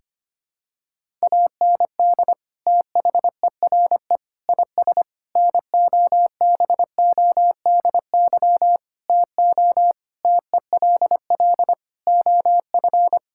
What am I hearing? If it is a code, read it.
Morse code, heard as a keyed tone, about 25 wpm, 700 Hz. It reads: AND THERE IS NOBODY TO TELL OF
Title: Sentences from Top 1600 Words 25wpm